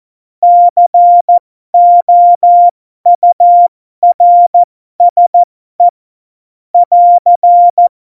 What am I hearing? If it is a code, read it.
Morse code, heard as a keyed tone, about 14 wpm, spaced as for 12 wpm, 700 Hz. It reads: COURSE <AR>
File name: Morsekurs / Witze